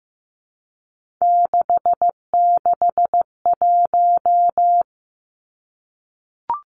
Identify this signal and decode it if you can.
Morse code, heard as a keyed tone, about 15 wpm, 700 Hz. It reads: 661
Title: Three Challenging Letter-Number Combinations 15wpm